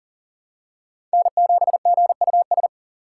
Morse code, heard as a keyed tone, about 40 wpm, 700 Hz. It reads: N7GUS
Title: USA Call Signs - 5 Characters - 3x Character Spacing - 40wpm